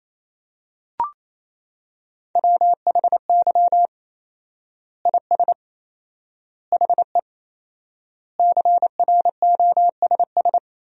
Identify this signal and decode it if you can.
Morse code, heard as a keyed tone, about 27 wpm, 700 Hz. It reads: WHY IS HE CROSS
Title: 4x Word-Spacing - Sentences from Top 300 Words - 28wpm